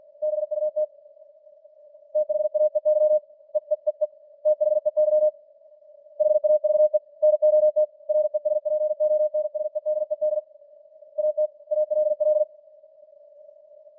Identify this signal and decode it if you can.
Morse code, heard as a keyed tone, about 45 wpm, 610 Hz. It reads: BUT THAEY EEEE THEY HAVE NOT RESPONSEDED AT ALL